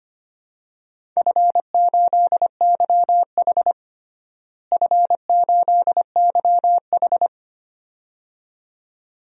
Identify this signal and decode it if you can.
Morse code, heard as a keyed tone, about 25 wpm, 700 Hz. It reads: F8YH F8YH
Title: DXCC-100 Call Signs - 4 Characters - Repeated 2x - 25wpm